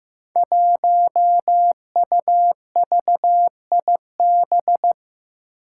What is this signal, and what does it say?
Morse code, heard as a keyed tone, about 15 wpm, 700 Hz. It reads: 1UVIB
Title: Five Challenging Letter-Number Combinations 15wpm